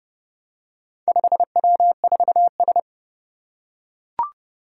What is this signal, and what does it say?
Morse code, heard as a keyed tone, about 30 wpm, 700 Hz. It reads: 5W4S
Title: Four Challenging Letter-Number Combinations 30wpm